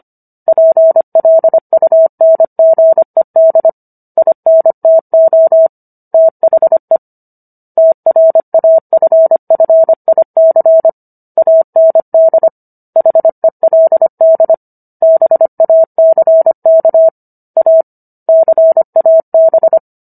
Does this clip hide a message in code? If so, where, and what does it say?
Yes: Morse code from 0.5 to 19.8 s, heard as a keyed tone, about 25 wpm, 650 Hz. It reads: PLUNGED INTO THE TRAFFIC AND HELD BACK A CAB